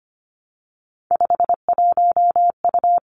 Morse code, heard as a keyed tone, about 25 wpm, 700 Hz. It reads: 51U
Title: Three Challenging Letter-Number Combinations 25wpm